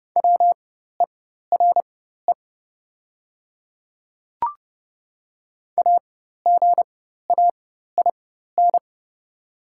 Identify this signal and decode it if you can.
Morse code, heard as a keyed tone, about 30 wpm, spaced as for 14 wpm, 700 Hz. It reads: WERE AGAIN
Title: 4x Character Spacing - Top 200 Words - 30wpm